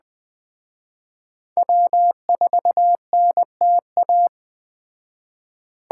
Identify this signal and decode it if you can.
Morse code, heard as a keyed tone, about 20 wpm, 700 Hz. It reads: W4NTA
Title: Speed Racing - USA Call Signs 17wpm